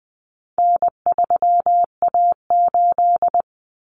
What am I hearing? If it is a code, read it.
Morse code, heard as a keyed tone, about 20 wpm, 700 Hz. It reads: N3A8
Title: Common Cut Numbers - 4-Digit 20wpm